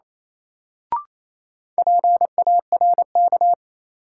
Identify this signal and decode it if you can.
Morse code, heard as a keyed tone, about 28 wpm, 700 Hz. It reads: PARK
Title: Sentences from Top 1300 Words - New Word Drill - 28wpm